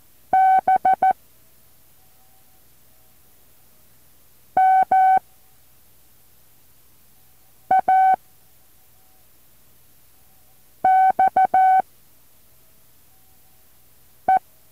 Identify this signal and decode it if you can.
Morse code, heard as a keyed tone, about 14 wpm, 760 Hz. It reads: B M A X E
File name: intl morse code2